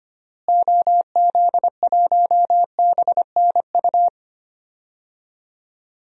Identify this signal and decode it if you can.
Morse code, heard as a keyed tone, about 25 wpm, 700 Hz. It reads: OZ1BNU
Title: DXCC-100 Call Signs - 6 Characters - Repeated 2x - 25wpm